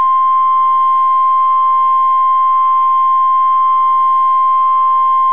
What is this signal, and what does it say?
Emergency Alert System (SAME) tones (the 1050 Hz attention tone). It is heard from 0.3 to 5.0 s.